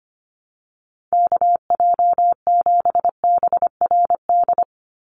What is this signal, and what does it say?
Morse code, heard as a keyed tone, about 25 wpm, 700 Hz. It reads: KJ7BRD
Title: USA Call Signs 25wpm